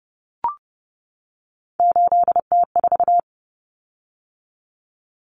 Morse code, heard as a keyed tone, about 30 wpm, 700 Hz. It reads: 8T4
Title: Common Cut Numbers - 3-Digit 30wpm